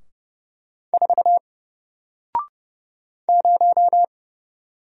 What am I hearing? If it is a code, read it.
Morse code, heard as a keyed tone, about 30 wpm, 700 Hz. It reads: 4 0
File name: Single-Letter-Number-30wpm